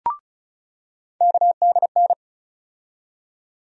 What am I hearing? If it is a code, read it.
Morse code, heard as a keyed tone, about 34 wpm, 700 Hz. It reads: KDN
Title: USA Call Sign Postfixes - 3 Letters 35wpm